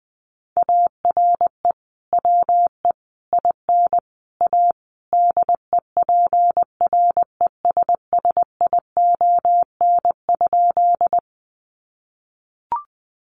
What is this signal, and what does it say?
Morse code, heard as a keyed tone, about 20 wpm, 700 Hz. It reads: ARE WE IN A DEPRESSION?
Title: Sentences from Top 1300 Words 20wpm